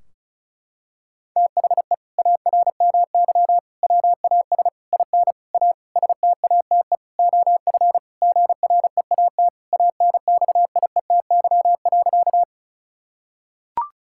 Morse code, heard as a keyed tone, about 36 wpm, 700 Hz. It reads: THE ARMY WAS IN A STATE OF GREAT ANXIETY.
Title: Sentences from Top 1600 Words 35wpm